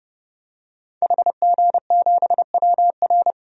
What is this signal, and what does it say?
Morse code, heard as a keyed tone, about 31 wpm, 700 Hz. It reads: HG7WR
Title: DXCC-100 Call Signs - 5 Characters - Repeated 2x - 30wpm